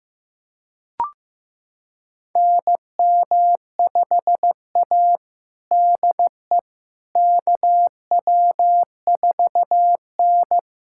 Morse code, heard as a keyed tone, about 16 wpm, 700 Hz. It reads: NM5A DE KW4N
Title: USA Callsign DE Callsign - 4 Characters - 15wpm